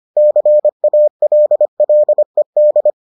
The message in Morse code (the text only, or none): CALLED